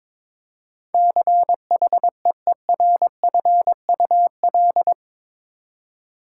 Morse code, heard as a keyed tone, about 22 wpm, 700 Hz. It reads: CHEERFUL
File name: practice / 22wpm / Eight Letter Words - 22wpm